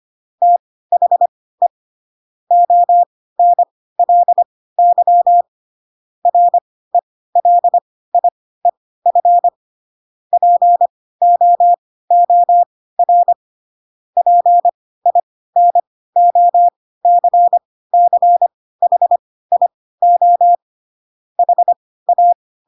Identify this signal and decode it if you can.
Morse code, heard as a keyed tone, about 25 wpm, spaced as for 16 wpm, 700 Hz. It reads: THE ONLY RELIEF POOR PINOCCHIO HA